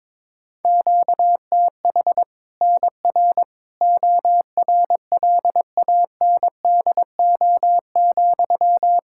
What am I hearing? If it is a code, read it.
Morse code, heard as a keyed tone, about 21 wpm, 700 Hz. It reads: QTH NR ORLANDO,
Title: Practice QSO - Exchange 1 - Beginning - 22wpm